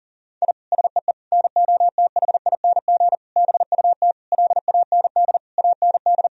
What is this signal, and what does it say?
Morse code, heard as a keyed tone, about 37 wpm, 700 Hz. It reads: I SEE NOTHING BUT LAND AND
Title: Sentences from Top 1000 Words - Review Part 1 - 40wpm